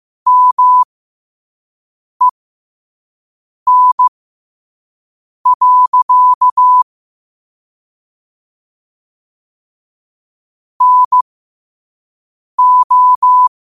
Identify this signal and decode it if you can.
Morse code, heard as a keyed tone, about 15 wpm, spaced as for 5 wpm, 1000 Hz. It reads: MEN. NO
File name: qotdfiles / 5wpm